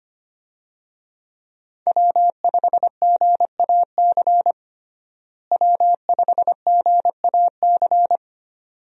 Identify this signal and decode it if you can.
Morse code, heard as a keyed tone, about 25 wpm, 700 Hz. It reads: W5GAC W5GAC
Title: USA Call Signs - 5 Characters - Repeated 2x - 25wpm